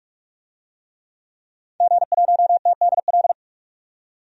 Morse code, heard as a keyed tone, about 42 wpm, 700 Hz. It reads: G1TDL
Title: DXCC-100 Call Signs - 5 Characters - 45wpm